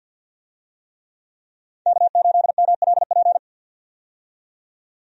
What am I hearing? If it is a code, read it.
Morse code, heard as a keyed tone, about 53 wpm, 700 Hz. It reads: K8MLP
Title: USA Call Signs - 5 Characters - 50wpm